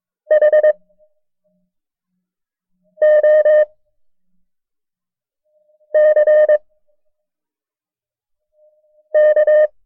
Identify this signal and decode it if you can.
Morse code, heard as a keyed tone, about 22 wpm, spaced as for 4 wpm, 610 Hz. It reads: HOCK
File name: FreeFiles / CW-practice / KB6MT-tape1-side2